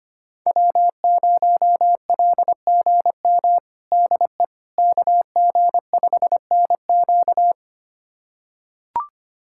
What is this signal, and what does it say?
Morse code, heard as a keyed tone, about 25 wpm, 700 Hz. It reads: W0LGM DE KG5NQ
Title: USA Callsign DE Callsign - 5 Characters - 25wpm